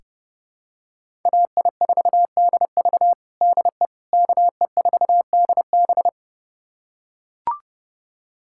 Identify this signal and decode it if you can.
Morse code, heard as a keyed tone, about 30 wpm, 700 Hz. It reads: AI4DV DE KE4DB
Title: USA Callsign DE Callsign - 5 Characters - 30wpm